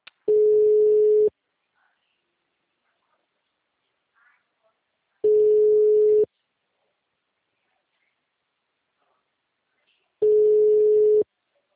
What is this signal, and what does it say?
Ringback tone (Europe, 425 Hz), with speech.